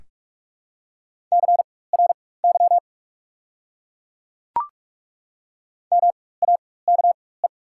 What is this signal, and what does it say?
Morse code, heard as a keyed tone, about 43 wpm, spaced as for 21 wpm, 700 Hz. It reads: CRY MAKE
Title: 4x Character Spacing - Top 500 Words - 45wpm